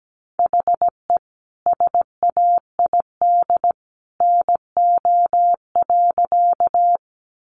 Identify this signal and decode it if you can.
Morse code, heard as a keyed tone, about 17 wpm, 700 Hz. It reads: HE SAID NO.
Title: Sentences from Top 100 Words 17wpm